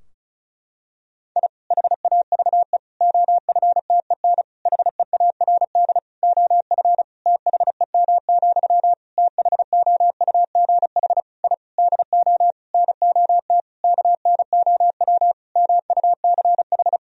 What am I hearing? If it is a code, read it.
Morse code, heard as a keyed tone, about 36 wpm, 700 Hz. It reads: I HAVE OFTEN HEARD OF THEM, THOUGH I DO NOT KNOW MUCH
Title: Sentences from Top 1000 Words - Review Part 2 - 35wpm